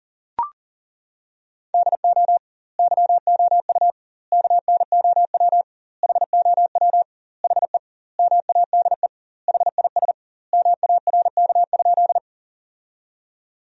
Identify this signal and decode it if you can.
Morse code, heard as a keyed tone, about 37 wpm, 700 Hz. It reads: DO YOU KNOW HOW HE MADE HIS MARK?
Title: 2x Word-Spacing - Sentences from Top 300 Words - 40wpm